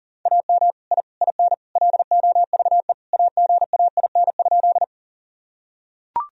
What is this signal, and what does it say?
Morse code, heard as a keyed tone, about 40 wpm, 700 Hz. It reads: AM I IN LOVE AGAIN?
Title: Sentences from Top 1000 Words - Review Part 2 - 40wpm